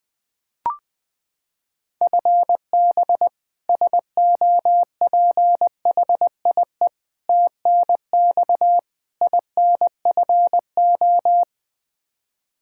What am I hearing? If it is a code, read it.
Morse code, heard as a keyed tone, about 20 wpm, 700 Hz. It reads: FB SOPHIE TNX INFO
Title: K1USN SST QSO Practice 20wpm